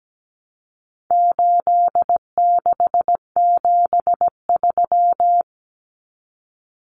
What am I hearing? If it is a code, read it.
Morse code, heard as a keyed tone, about 17 wpm, 700 Hz. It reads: 8673
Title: Common Cut Numbers - 4-Digit 17wpm